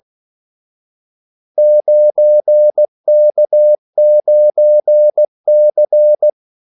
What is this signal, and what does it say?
Morse code, heard as a keyed tone, about 16 wpm, 600 Hz. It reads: 9K9C
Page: Kaldesignaler 16 wpm